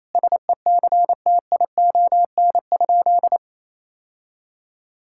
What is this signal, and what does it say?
Morse code, heard as a keyed tone, about 28 wpm, 700 Hz. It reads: SECTION?
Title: Sentences from Top 1000 Words - Review Part 5 - 28wpm